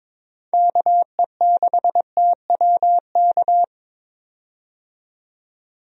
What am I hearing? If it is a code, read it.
Morse code, heard as a keyed tone, about 22 wpm, 700 Hz. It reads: KE6TWK
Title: USA Call Signs - 6 Characters - 22wpm